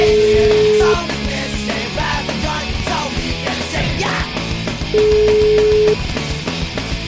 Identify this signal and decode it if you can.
Ringback tone (Europe, 425 Hz), with noise.